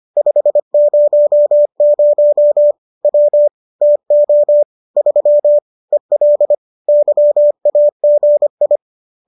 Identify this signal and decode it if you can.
Morse code, heard as a keyed tone, about 25 wpm, 590 Hz. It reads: 500 W TO 3 EL YAGI